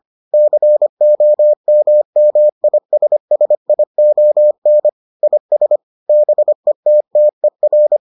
Morse code, heard as a keyed tone, about 25 wpm, 600 Hz. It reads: COMMISSION IS BETTER